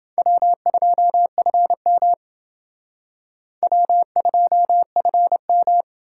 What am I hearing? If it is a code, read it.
Morse code, heard as a keyed tone, about 28 wpm, 700 Hz. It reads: W2FM W2FM